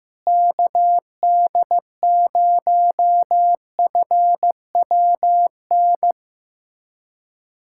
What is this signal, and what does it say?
Morse code, heard as a keyed tone, about 16 wpm, 700 Hz. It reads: KD0FWN
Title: USA Callsign DE Callsign - 6 Characters - 15wpm